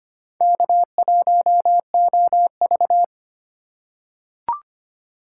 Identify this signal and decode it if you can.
Morse code, heard as a keyed tone, about 25 wpm, 700 Hz. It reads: K1OV
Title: USA Call Signs - 4 Characters - 2x Character Spacing - 25wpm